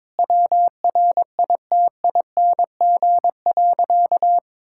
Morse code, heard as a keyed tone, about 22 wpm, 700 Hz. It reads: WRITING.
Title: Sentences from Top 1600 Words 22wpm